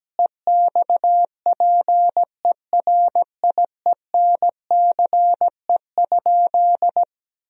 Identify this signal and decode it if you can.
Morse code, heard as a keyed tone, about 17 wpm, 700 Hz. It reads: EXPERIENCE?
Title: Sentences from Top 1000 Words - Review Part 4 - 17wpm